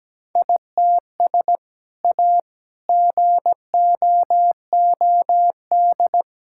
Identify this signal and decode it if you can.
Morse code, heard as a keyed tone, about 17 wpm, 700 Hz. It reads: ITS A GOOD
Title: Sentences from Top 1000 Words - Review Part 5 - 17wpm